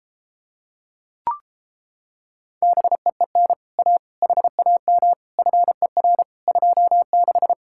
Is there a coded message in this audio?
Yes: Morse code from 2.6 to 7.5 s, heard as a keyed tone, about 29 wpm, 700 Hz. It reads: BEEN A HAM FER 26